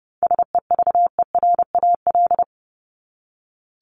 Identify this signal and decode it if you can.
Morse code, heard as a keyed tone, about 30 wpm, 700 Hz. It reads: SEVERAL
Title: Top 500 Words 30wpm